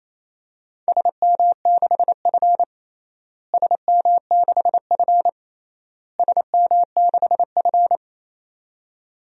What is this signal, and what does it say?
Morse code, heard as a keyed tone, about 28 wpm, 700 Hz. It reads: SM6F SM6F SM6F
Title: DXCC-100 Call Signs - 4 Characters - Repeated 3x - 28wpm